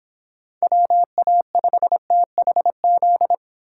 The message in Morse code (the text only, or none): WA5THZ